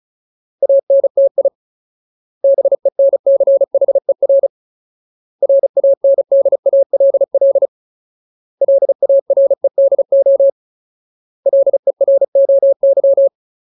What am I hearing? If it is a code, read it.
Morse code, heard as a keyed tone, about 34 wpm, 550 Hz. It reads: ANTI BENCHER RANDALL LAREDO LEROY
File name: Ultimate Word List / 35WPM Ultimate